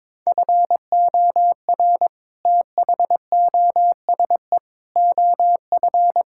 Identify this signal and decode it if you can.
Morse code, heard as a keyed tone, about 22 wpm, 700 Hz. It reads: FOR THOSE OF
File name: practice / 22wpm / Sets of 7 Words - 22wpm